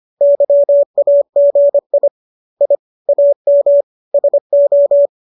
Morse code, heard as a keyed tone, about 25 wpm, 570 Hz. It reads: YAGI I AM SO